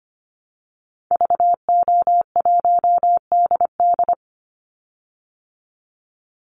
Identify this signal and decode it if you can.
Morse code, heard as a keyed tone, about 25 wpm, 700 Hz. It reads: VO1DD
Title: Call Signs Repeated 3x - 25wpm